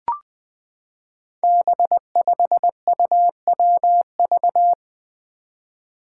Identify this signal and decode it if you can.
Morse code, heard as a keyed tone, about 20 wpm, 700 Hz. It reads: B5UWV
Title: Five Challenging Letter-Number Combinations 20wpm